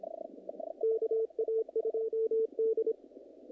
Morse code, heard as a keyed tone, about 31 wpm, 450 Hz. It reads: KA2D
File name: contests / naqp-cw-k5tr-op-k5pi- -2000